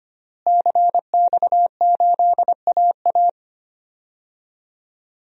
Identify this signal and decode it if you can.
Morse code, heard as a keyed tone, about 25 wpm, 700 Hz. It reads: CX8AA
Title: DXCC-100 Call Signs - 5 Characters - Repeated 3x - 25wpm